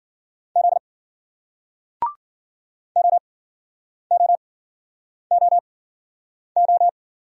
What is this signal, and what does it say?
Morse code, heard as a keyed tone, about 55 wpm, spaced as for 9 wpm, 700 Hz. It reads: Z OOOO